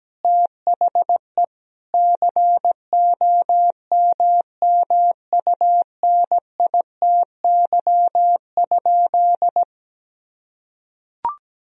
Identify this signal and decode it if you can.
Morse code, heard as a keyed tone, about 17 wpm, 700 Hz. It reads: THE COMMUNITY?
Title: Sentences from Top 1000 Words - Review Part 4 - 17wpm